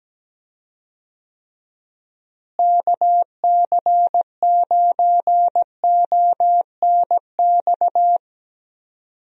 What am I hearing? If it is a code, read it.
Morse code, heard as a keyed tone, about 17 wpm, 700 Hz. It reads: KC9ONX
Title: USA Call Signs - 6 Characters - Repeated 3x - 17wpm